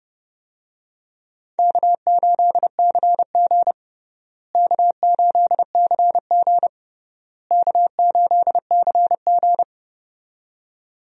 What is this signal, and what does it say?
Morse code, heard as a keyed tone, about 32 wpm, 700 Hz. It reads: K8CG K8CG K8CG
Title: USA Call Signs - 4 Characters - Repeated 3x - 30wpm